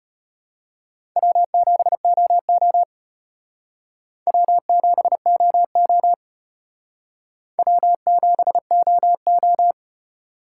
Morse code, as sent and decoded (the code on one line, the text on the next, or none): .-- --... --- --- / .-- --... --- --- / .-- --... --- ---
W7OO W7OO W7OO